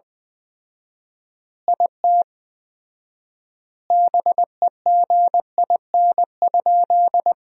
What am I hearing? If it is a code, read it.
Morse code, heard as a keyed tone, about 20 wpm, 700 Hz. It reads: IT BEGIN?
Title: 4x Word-Spacing - Sentences from Top 300 Words - 20wpm